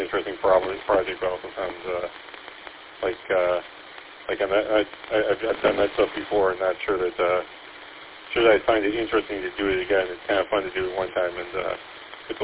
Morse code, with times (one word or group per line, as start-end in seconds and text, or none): none